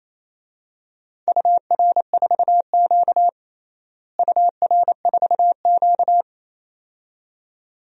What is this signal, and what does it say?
Morse code, heard as a keyed tone, about 28 wpm, 700 Hz. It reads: UR4Q UR4Q
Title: DXCC-100 Call Signs - 4 Characters - Repeated 2x - 28wpm